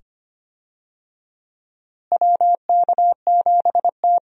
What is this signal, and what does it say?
Morse code, heard as a keyed tone, about 25 wpm, 700 Hz. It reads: WK7T
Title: USA Call Signs - 4 Characters - Repeated 2x - 25wpm